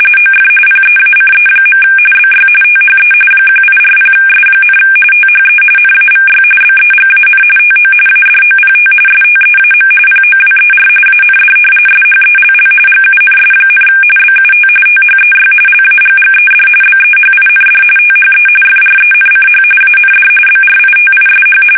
RTTY, about 45.45 baud. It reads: OF PEACE. BY THE 1840S, THE NAVY BEGAN TO ADOPT STEAM POWER AND SHELL GUNS, BUT THEY LAGGED BEHIND THE FRENCH AND BRITISH
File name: fsk850 / usnavy2